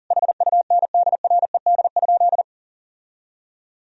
Morse code, heard as a keyed tone, about 40 wpm, 700 Hz. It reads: HUNDRED?
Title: Sentences from Top 1000 Words - Review Part 2 - 40wpm